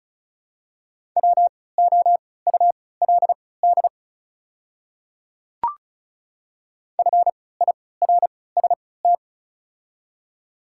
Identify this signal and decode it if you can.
Morse code, heard as a keyed tone, about 36 wpm, spaced as for 20 wpm, 700 Hz. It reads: WOULD FIRST
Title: 3x Character Spacing - Top 200 Words - 35wpm